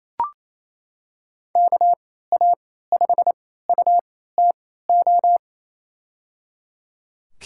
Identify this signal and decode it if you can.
Morse code, heard as a keyed tone, about 28 wpm, spaced as for 16 wpm, 700 Hz. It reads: KA5UTO
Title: USA Call Signs - 6 Characters - 3x Character Spacing - 28wpm